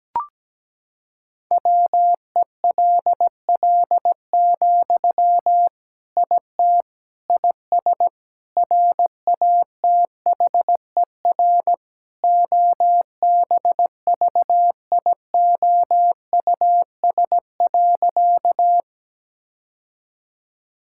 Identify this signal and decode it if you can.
Morse code, heard as a keyed tone, about 17 wpm, 700 Hz. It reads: WELL, IT IS RATHER OBVIOUS.
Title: Sentences from Top 1600 Words 17wpm